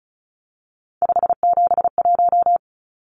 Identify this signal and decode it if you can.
Morse code, heard as a keyed tone, about 34 wpm, 700 Hz. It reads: 571
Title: Numbers - 3-Digit 35wpm